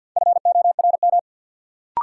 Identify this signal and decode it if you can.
Morse code, heard as a keyed tone, about 47 wpm, 700 Hz. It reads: FORM
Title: Sets of 4 Words - New Word Drill - 50wpm